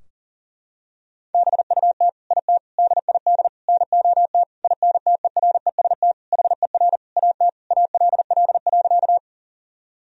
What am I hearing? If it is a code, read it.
Morse code, heard as a keyed tone, about 40 wpm, 700 Hz. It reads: BUT IT DID NOT INTEREST HER AT ALL.
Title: Sentences from Top 1000 Words - Review Part 2 - 40wpm